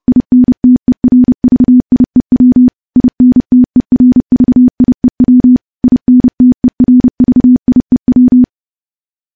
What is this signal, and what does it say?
Morse code, heard as a keyed tone, about 30 wpm, 250 Hz. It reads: INTERVIEW INTERVIEW INTERVIEW
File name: morse / morse-training-30WPM